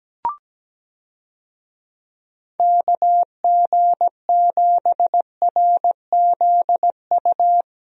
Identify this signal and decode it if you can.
Morse code, heard as a keyed tone, about 17 wpm, 700 Hz. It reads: KG7RZU
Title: USA Call Signs - 6 Characters - Repeated 3x - 17wpm